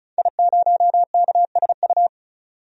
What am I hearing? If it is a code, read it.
Morse code, heard as a keyed tone, about 34 wpm, 700 Hz. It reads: I0KSU
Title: DXCC-100 Call Signs - 5 Characters - 35wpm